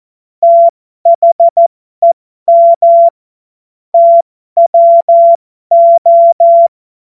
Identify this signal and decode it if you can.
Morse code, heard as a keyed tone, about 14 wpm, spaced as for 12 wpm, 680 Hz. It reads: THEM TWO
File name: Morsekurs / Witze